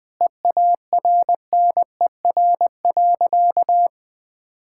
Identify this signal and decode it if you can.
Morse code, heard as a keyed tone, about 20 wpm, 700 Hz. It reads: EARNER.
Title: Pangram Challenge v2 - 20wpm